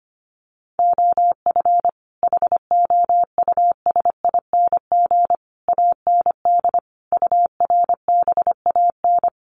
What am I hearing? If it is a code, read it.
Morse code, heard as a keyed tone, about 25 wpm, 700 Hz. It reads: OF HOUSING AND URBAN
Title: Sets of 5 Words Encore2 25wpm